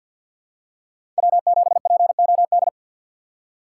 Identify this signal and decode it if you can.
Morse code, heard as a keyed tone, about 49 wpm, 700 Hz. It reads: W7POD
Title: USA Call Signs - 5 Characters - 4x Character Spacing - 50wpm